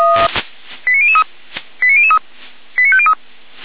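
Five-tone selective call (ZVEI), heard as quick runs of five tones, tones of about 70 ms. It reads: 89003 89003 89503
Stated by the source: ZVEI five-tone selective call 89099, received with an SDR, from the Feuerwehr Orlamünde audio server.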